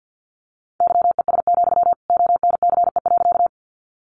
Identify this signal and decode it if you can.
Morse code, heard as a keyed tone, about 50 wpm, 700 Hz. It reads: YES, ONCE.
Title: Sentences from Top 500 Words 50wpm